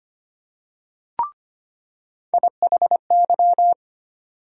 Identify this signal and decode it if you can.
Morse code, heard as a keyed tone, about 25 wpm, 700 Hz. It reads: IHY
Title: USA Call Sign Postfixes - 3 Letters 25wpm